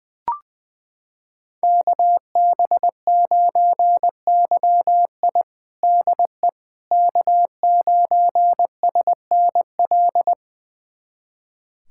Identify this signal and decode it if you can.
Morse code, heard as a keyed tone, about 20 wpm, 700 Hz. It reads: KB9YI DE K9SNL
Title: USA Callsign DE Callsign - 5 Characters - 20wpm